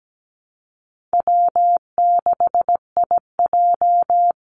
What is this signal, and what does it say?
Morse code, heard as a keyed tone, about 17 wpm, 700 Hz. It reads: W6IJ
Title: Easy Call Signs - 17wpm